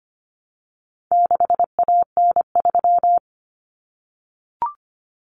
Morse code, heard as a keyed tone, about 25 wpm, 700 Hz. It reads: 6AN3
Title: Common Cut Numbers - 4-Digit 25wpm